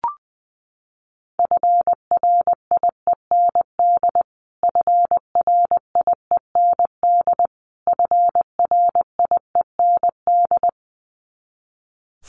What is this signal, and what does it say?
Morse code, heard as a keyed tone, about 20 wpm, 700 Hz. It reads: FRIEND FRIEND FRIEND
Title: Top 500 Words Repeated 20wpm